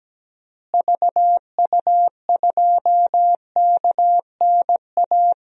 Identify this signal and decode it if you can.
Morse code, heard as a keyed tone, about 17 wpm, 700 Hz. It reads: VU2KNA
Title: DXCC-100 Call Signs - 6 Characters - Repeated 2x - 17wpm